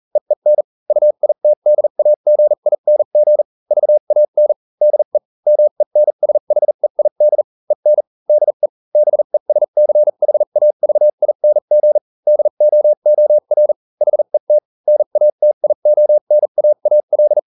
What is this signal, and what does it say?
Morse code, heard as a keyed tone, about 40 wpm, spaced as for 35 wpm, 600 Hz. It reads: EEN UITDAGING VAN DE MENSHEID EN DE BESCHAVING DOOR HET NATIONAAL